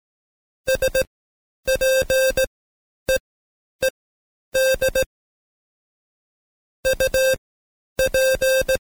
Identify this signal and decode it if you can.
Morse code, heard as a keyed tone, about 17 wpm, spaced as for 9 wpm, 530 Hz. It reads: SPEED UP
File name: Code Quick / AudioDownld2 / Seg131-Stepper4 9WPM